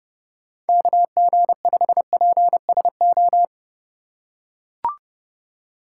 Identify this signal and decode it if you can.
Morse code, heard as a keyed tone, about 31 wpm, 700 Hz. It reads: KG5PSO
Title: USA Call Signs - 6 Characters - 4x Character Spacing - 30wpm